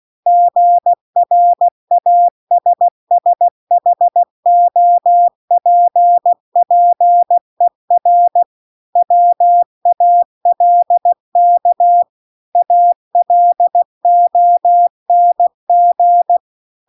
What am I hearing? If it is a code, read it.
Morse code, heard as a keyed tone, about 16 wpm, 700 Hz. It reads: GRASSHOPPER WALK ALONG